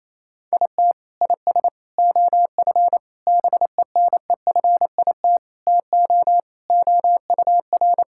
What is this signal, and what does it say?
Morse code, heard as a keyed tone, about 28 wpm, 700 Hz. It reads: IT IS OF BENEFIT TO OUR
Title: Sentences from Top 1000 Words - Review Part 6 - 28wpm